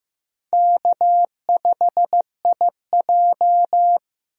Morse code, heard as a keyed tone, about 15 wpm, 700 Hz. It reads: K5IJ
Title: USA Call Signs - 4 Characters - Repeated 2x - 15wpm